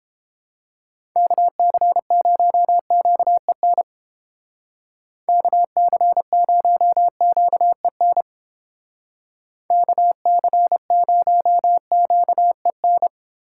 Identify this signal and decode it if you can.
Morse code, heard as a keyed tone, about 29 wpm, 700 Hz. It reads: KC0QEN KC0QEN KC0QEN